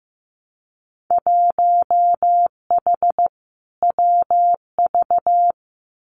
Morse code, heard as a keyed tone, about 15 wpm, 700 Hz. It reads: 1H WV
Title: Winter Field Day Exchanges Encore 15wpm